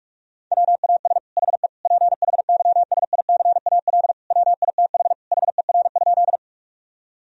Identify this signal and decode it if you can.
Morse code, heard as a keyed tone, about 46 wpm, 700 Hz. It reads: WAS HE PHYSICAL WITH HER?
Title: Sentences from Top 1000 Words - Review Part 4 - 45wpm